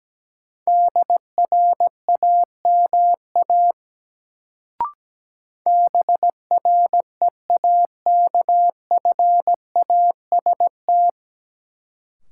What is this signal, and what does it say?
Morse code, heard as a keyed tone, about 17 wpm, 700 Hz. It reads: DRAMA BREAKFAST
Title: Sentences from Top 1500 Words - New Word Drill Alt - 17wpm